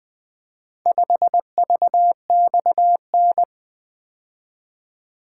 Morse code, heard as a keyed tone, about 24 wpm, 700 Hz. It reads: 5VXN
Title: Speed-Racing - Four Letter-Number Combinations - 20wpm